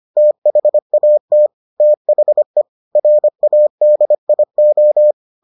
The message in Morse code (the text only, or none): THAT THE RADIO